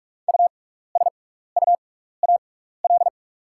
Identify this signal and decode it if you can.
Morse code, heard as a keyed tone, about 45 wpm, spaced as for 15 wpm, 700 Hz. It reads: USUAL
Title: 6x Character Spacing - Top 500 Words - 45wpm